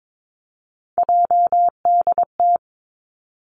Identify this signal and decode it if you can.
Morse code, heard as a keyed tone, about 22 wpm, 700 Hz. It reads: JDT
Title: Three Challenging Letter-Number Combinations 22wpm